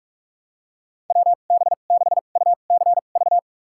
Morse code, heard as a keyed tone, about 46 wpm, spaced as for 33 wpm, 700 Hz. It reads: WB6UCV
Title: USA Call Signs - 6 Characters - 2x Character Spacing - 45wpm